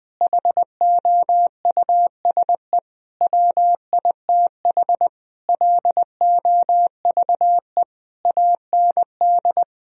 Morse code, heard as a keyed tone, about 19 wpm, 700 Hz. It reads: HOUSE WITH LOVE AND
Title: Sentences from Top 1000 Words - Review Part 3 - 20wpm